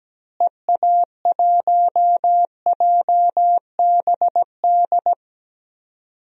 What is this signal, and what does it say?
Morse code, heard as a keyed tone, about 17 wpm, 700 Hz. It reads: EA1JBD
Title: DXCC-100 Call Signs - 6 Characters - Repeated 2x - 17wpm